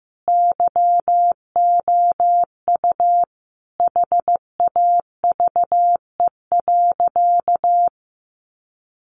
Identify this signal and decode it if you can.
Morse code, heard as a keyed tone, about 14 wpm, 700 Hz. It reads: YOU HAVE.
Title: Sentences from Top 1300 Words 15wpm